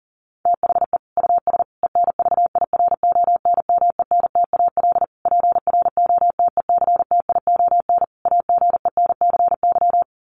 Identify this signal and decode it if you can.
Morse code, heard as a keyed tone, about 40 wpm, 700 Hz. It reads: THE US ENVIRONMENTAL PROTECTION AGENCY
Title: Sets of 5 Words Encore2 40wpm